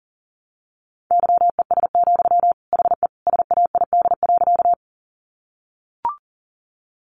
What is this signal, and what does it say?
Morse code, heard as a keyed tone, about 40 wpm, 700 Hz. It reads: YES, HE SAID.
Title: Sentences from Top 500 Words 40wpm